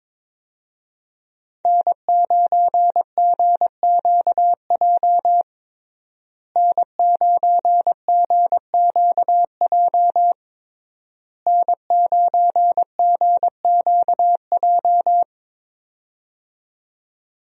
Morse code, heard as a keyed tone, about 22 wpm, 700 Hz. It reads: N9GQJ N9GQJ N9GQJ
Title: USA Call Signs - 5 Characters - Repeated 3x - 22wpm